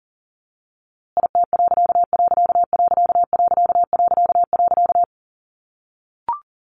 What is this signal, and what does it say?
Morse code, heard as a keyed tone, about 37 wpm, 700 Hz. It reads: IT......
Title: A Sound Of Thunder 40wpm